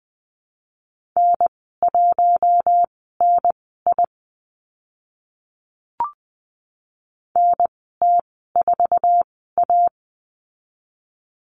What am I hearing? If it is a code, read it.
Morse code, heard as a keyed tone, about 20 wpm, spaced as for 14 wpm, 700 Hz. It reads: N1NI NT4A
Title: 076 Lesson 13g - Letter L - Call Signs 20wpm